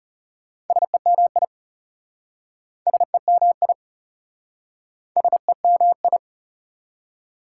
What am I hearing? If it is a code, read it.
Morse code, heard as a keyed tone, about 37 wpm, 690 Hz. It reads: SEMI SEMI SEMI
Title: Speed-Racing Prefixes and Suffixes 30wpm